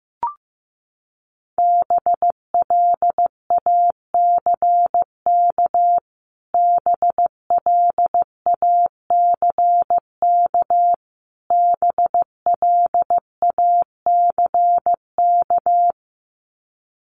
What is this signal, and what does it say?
Morse code, heard as a keyed tone, about 15 wpm, 700 Hz. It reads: BLACK BLACK BLACK
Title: Top 500 Words Repeated 15wpm